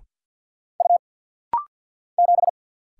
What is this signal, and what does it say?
Morse code, heard as a keyed tone, about 50 wpm, 700 Hz. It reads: U 7
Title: Single Letter-Number - Rapid-Fire 50wpm